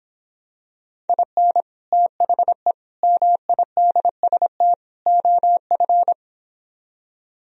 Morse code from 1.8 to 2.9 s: THE

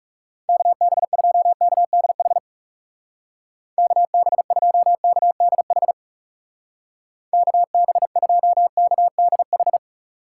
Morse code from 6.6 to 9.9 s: KB2KDH

